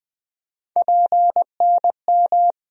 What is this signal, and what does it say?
Morse code, heard as a keyed tone, about 21 wpm, 700 Hz. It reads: PNM
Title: USA Call Sign Postfixes - 3 Letters 20wpm